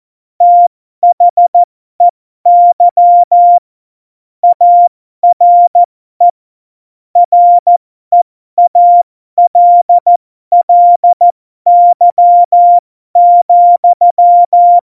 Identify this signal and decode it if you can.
Morse code, heard as a keyed tone, about 14 wpm, spaced as for 12 wpm, 700 Hz. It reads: THEY ARE REALLY,
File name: Morsekurs / Witze